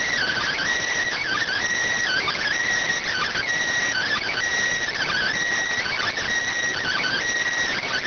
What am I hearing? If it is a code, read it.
Emergency Alert System (SAME) tones (header bursts), heard as harsh buzzing bursts. It is heard from 1.2 to 7.2 s.